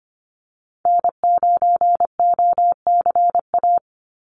Morse code, heard as a keyed tone, about 25 wpm, 700 Hz. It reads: N9OCA
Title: USA Call Signs 25wpm